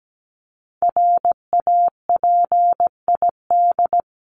Morse code, heard as a keyed tone, about 17 wpm, 700 Hz. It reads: RAPID
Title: LY Suffix - Top 100 Words with Base Words 17wpm